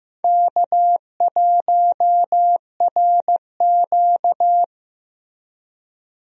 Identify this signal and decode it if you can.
Morse code, heard as a keyed tone, about 15 wpm, 700 Hz. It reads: K1RQ
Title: USA Call Signs - 4 Characters - Repeated 2x - 15wpm